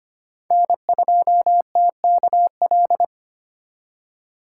Morse code, heard as a keyed tone, about 25 wpm, 700 Hz. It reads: N2TKL
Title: USA Call Signs - 5 Characters - 4x Character Spacing - 25wpm